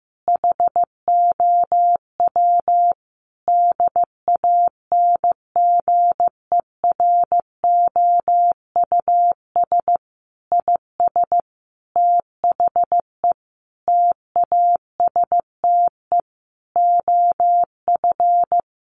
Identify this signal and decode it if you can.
Morse code, heard as a keyed tone, about 15 wpm, 700 Hz. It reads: HOW DANGEROUS IS THE TASTE OF
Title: Sentences from Top 1200 Words 15wpm